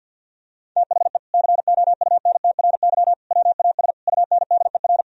Morse code, heard as a keyed tone, about 51 wpm, 700 Hz. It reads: THE COUNTRY WAS UNDER
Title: Sentences from Top 1000 Words - Review Part 1 - 50wpm